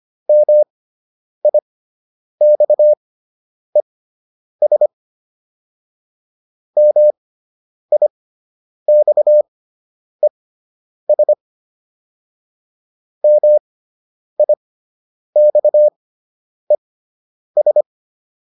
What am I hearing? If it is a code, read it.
Morse code, heard as a keyed tone, about 25 wpm, spaced as for 9 wpm, 600 Hz. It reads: MIXES MIXES MIXES